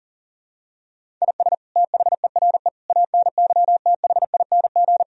Morse code, heard as a keyed tone, about 40 wpm, 700 Hz. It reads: IS THERE ANYTHING
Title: Sentences from Top 1000 Words - Review Part 4 - 40wpm